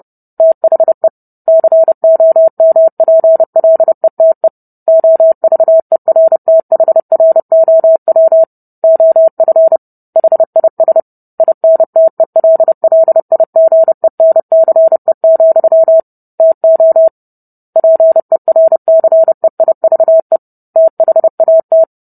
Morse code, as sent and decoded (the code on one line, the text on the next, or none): - .... / -.-. --- -- .--. .-.. - . / --- ...- . .-. - .... .-. --- .-- / --- ..-. / .... .. ... / .. -. - . .-.. .-.. .. --. -. -.-. --..-- / - --- / .--. .-. -.-. .. ...- . / - .... .- -
THE COMPLETE OVERTHROW OF HIS INTELLIGENCE, TO PERCEIVE THAT